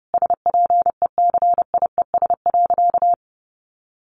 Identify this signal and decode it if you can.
Morse code, heard as a keyed tone, about 30 wpm, 700 Hz. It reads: SPECIES.
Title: A Sound Of Thunder 30wpm